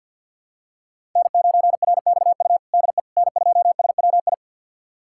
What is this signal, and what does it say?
Morse code, heard as a keyed tone, about 51 wpm, 700 Hz. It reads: N9RXU DE N2SWI
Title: USA Callsign DE Callsign - 5 Characters - 50wpm